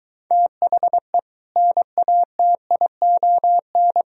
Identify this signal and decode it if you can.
Morse code, heard as a keyed tone, about 22 wpm, 700 Hz. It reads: THE NATION